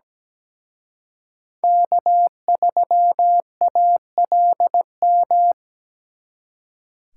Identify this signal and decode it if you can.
Morse code, heard as a keyed tone, about 20 wpm, 700 Hz. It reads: K3ALM
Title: Speed Racing - USA 1x3 Call Signs 17wpm